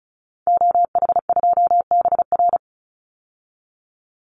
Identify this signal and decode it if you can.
Morse code, heard as a keyed tone, about 34 wpm, 700 Hz. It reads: OH2BR
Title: Call Signs Repeated 3x - 35wpm